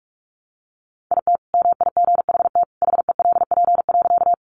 Morse code, heard as a keyed tone, about 44 wpm, 700 Hz. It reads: IT MIGHT HELP.
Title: Sentences from Top 300 Words 45wpm